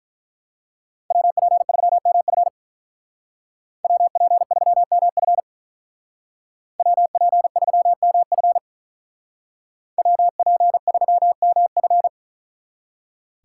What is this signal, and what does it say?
Morse code, heard as a keyed tone, about 44 wpm, 700 Hz. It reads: WP3MF WP3MF WP3MF WP3MF
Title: Speed Racing - USA 2x2 Call Signs 35wpm